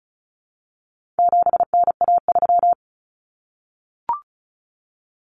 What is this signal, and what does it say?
Morse code, heard as a keyed tone, about 34 wpm, 700 Hz. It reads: 7NA3
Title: Common Cut Numbers - 4-Digit 35wpm